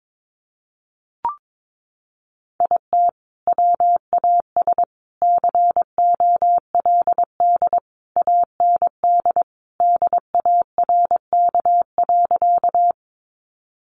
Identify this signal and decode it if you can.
Morse code, heard as a keyed tone, about 22 wpm, 700 Hz. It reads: IT WAS COLD AND DARK.
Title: Sentences from Top 500 Words 22wpm